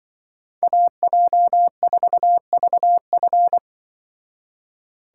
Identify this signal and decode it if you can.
Morse code, heard as a keyed tone, about 24 wpm, 700 Hz. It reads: AJ4VF
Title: Speed Racing - USA 2x2 Call Signs 20wpm